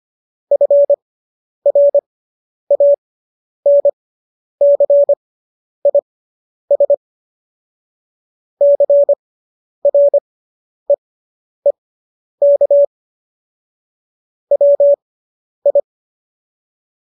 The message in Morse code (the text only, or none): FRANCIS CREEK WI